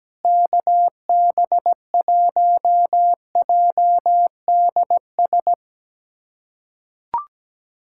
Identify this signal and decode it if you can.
Morse code, heard as a keyed tone, about 17 wpm, 700 Hz. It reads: KB1JDS
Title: USA Callsign DE Callsign 17wpm